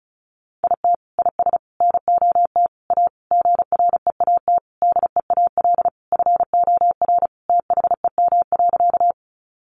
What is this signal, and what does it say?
Morse code, heard as a keyed tone, about 34 wpm, 700 Hz. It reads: IT IS NOT A GREAT DEAL FOR THEM.
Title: Sentences from Top 800 Words 35wpm